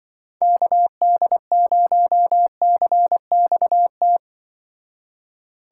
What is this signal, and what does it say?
Morse code, heard as a keyed tone, about 23 wpm, 700 Hz. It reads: KD0CXT
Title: Speed Racing - USA 2x3 Call Signs 20wpm